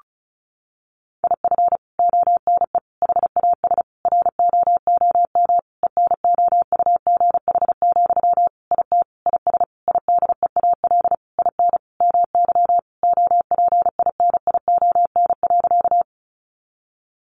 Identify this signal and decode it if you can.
Morse code, heard as a keyed tone, about 34 wpm, 700 Hz. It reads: IF ONE HAS ROOM ENOUGH, IT IS IDEAL IN MY OPINION.
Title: Sentences from Top 1400 Words 35wpm